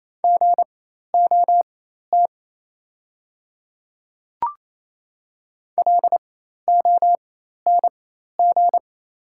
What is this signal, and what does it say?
Morse code, heard as a keyed tone, about 28 wpm, spaced as for 13 wpm, 700 Hz. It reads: GOT LONG
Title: 4x Character Spacing - Top 300 Words - 28wpm